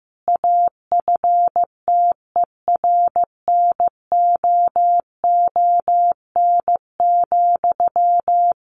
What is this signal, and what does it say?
Morse code, heard as a keyed tone, about 15 wpm, 700 Hz. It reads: AFTERNOON,
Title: Sentences from Top 1500 Words 15wpm